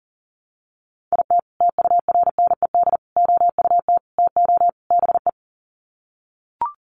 Morse code, heard as a keyed tone, about 37 wpm, 700 Hz. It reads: IT TURNED OUT TO BE
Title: Sets of 5 Words 40wpm